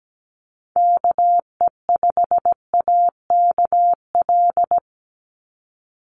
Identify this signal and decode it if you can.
Morse code, heard as a keyed tone, about 17 wpm, 700 Hz. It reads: KE5AKL
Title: NA SOTA Activator Callsigns 17wpm